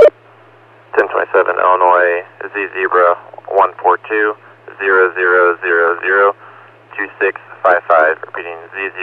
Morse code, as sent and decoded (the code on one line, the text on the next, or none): none